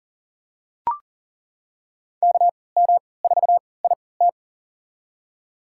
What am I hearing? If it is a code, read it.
Morse code, heard as a keyed tone, about 40 wpm, spaced as for 23 wpm, 700 Hz. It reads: KM4IT
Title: USA Call Signs - 3x Character Spacing - 40wpm